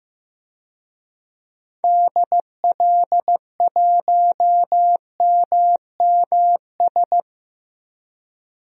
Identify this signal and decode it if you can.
Morse code, heard as a keyed tone, about 15 wpm, 700 Hz. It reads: DL1MMS
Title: DXCC-100 Call Signs - 6 Characters - 15wpm